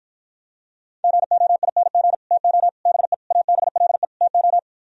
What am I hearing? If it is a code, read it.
Morse code, heard as a keyed tone, about 53 wpm, 700 Hz. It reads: GOING TO BE ABLE TO